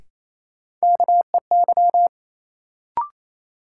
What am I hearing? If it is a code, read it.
Morse code, heard as a keyed tone, about 28 wpm, 700 Hz. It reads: KEY
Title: Sentences from Top 600 Words - New Word Drill - 28wpm